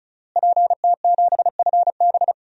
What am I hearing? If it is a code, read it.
Morse code, heard as a keyed tone, about 35 wpm, 700 Hz. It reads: PT7FB
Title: DXCC-100 Call Signs - 5 Characters - Repeated 2x - 35wpm